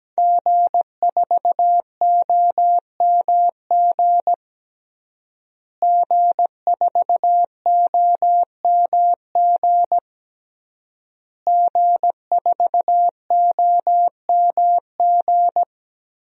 Morse code, heard as a keyed tone, about 17 wpm, 700 Hz. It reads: G4OMG G4OMG G4OMG
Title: DXCC-100 Call Signs - 5 Characters - Repeated 3x - 17wpm